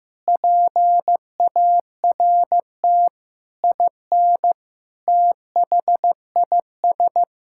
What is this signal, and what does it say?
Morse code, heard as a keyed tone, about 15 wpm, 700 Hz. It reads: PART IN THIS
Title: Sentences from Top 1000 Words - Review Part 6 - 15wpm